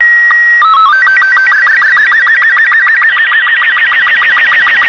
SSTV (slow-scan television), Robot 36.